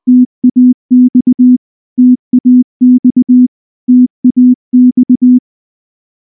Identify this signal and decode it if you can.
Morse code, heard as a keyed tone, about 19 wpm, 250 Hz. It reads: TAX TAX TAX
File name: morse / morse-training-20WPM-3